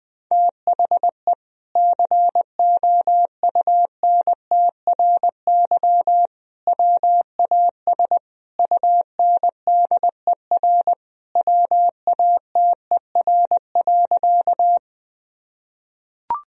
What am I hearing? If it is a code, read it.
Morse code, heard as a keyed tone, about 20 wpm, 700 Hz. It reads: THE COUNTRY WAS UNDER WATER.
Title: Sentences from Top 1000 Words - Review Part 1 - 20wpm